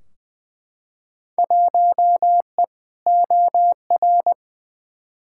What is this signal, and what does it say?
Morse code, heard as a keyed tone, about 21 wpm, 700 Hz. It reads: 1E OR
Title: Field Day Exchanges Encore - 20wpm